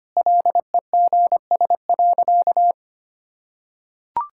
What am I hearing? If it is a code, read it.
Morse code, heard as a keyed tone, about 25 wpm, 690 Hz. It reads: LEGS.
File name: Easy Sentences with Hints - 25wpm